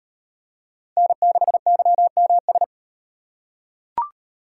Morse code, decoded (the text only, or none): N6YMS